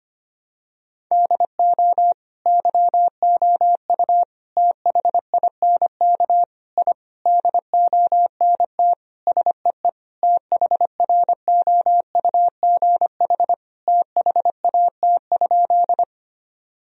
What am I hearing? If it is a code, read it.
Morse code, heard as a keyed tone, about 25 wpm, 700 Hz. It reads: DO YOU THINK I DONT SEE THROUGH THAT?
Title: Sentences from Top 1000 Words - Review Part 1 - 25wpm